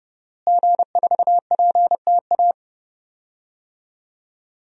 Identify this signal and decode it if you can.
Morse code, heard as a keyed tone, about 32 wpm, 700 Hz. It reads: G4PTA
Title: DXCC-100 Call Signs - 5 Characters - Repeated 3x - 30wpm